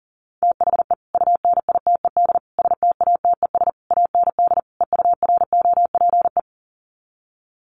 Morse code, heard as a keyed tone, about 40 wpm, 700 Hz. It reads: THE UNITED STATES AND EUROPE
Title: Sets of 5 Words Encore 40wpm